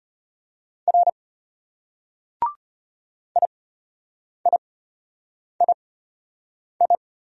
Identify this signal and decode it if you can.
Morse code, heard as a keyed tone, about 33 wpm, spaced as for 8 wpm, 700 Hz. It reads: R IIII